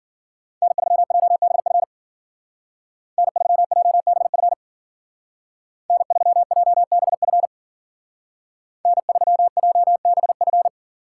Morse code, heard as a keyed tone, about 52 wpm, 700 Hz. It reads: N3JBF N3JBF N3JBF N3JBF